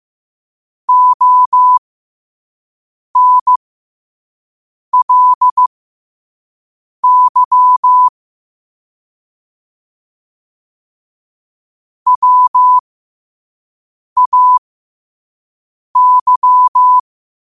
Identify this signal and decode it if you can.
Morse code, heard as a keyed tone, about 15 wpm, spaced as for 5 wpm, 1000 Hz. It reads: ONLY WAY
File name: qotdfiles / 5wpm